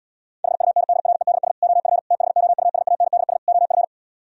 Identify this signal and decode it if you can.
Morse code, heard as a keyed tone, about 77 wpm, 700 Hz. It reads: HUNDREDS OF THOUSANDS OF